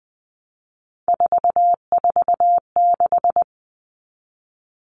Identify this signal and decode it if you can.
Morse code, heard as a keyed tone, about 20 wpm, 700 Hz. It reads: 446
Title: Numbers - 3-Digit 20wpm